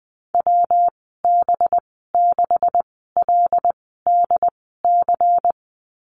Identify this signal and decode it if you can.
Morse code, heard as a keyed tone, about 20 wpm, spaced as for 15 wpm, 700 Hz. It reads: WB6LDC
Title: 193 Lesson 31g - Letter v - Call Signs 20wpm